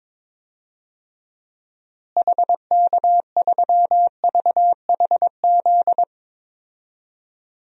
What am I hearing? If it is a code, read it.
Morse code, heard as a keyed tone, about 21 wpm, 700 Hz. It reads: HK3VHZ
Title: DXCC-100 Call Signs - 6 Characters - Repeated 2x - 22wpm